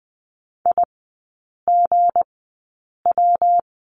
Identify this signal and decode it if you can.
Morse code, heard as a keyed tone, about 19 wpm, spaced as for 8 wpm, 700 Hz. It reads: IGW
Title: 159 Lesson 26e - Letter g - Three Characters 20wpm